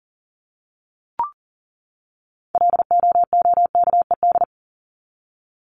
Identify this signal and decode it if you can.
Morse code, heard as a keyed tone, about 40 wpm, 700 Hz. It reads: LOOKED
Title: ED Suffix - Top 100 Words with Base Words 40wpm